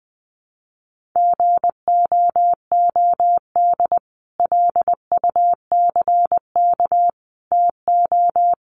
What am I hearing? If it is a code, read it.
Morse code, heard as a keyed tone, about 20 wpm, 700 Hz. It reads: GOOD LUCK TO
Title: Sentences from Top 1500 Words 20wpm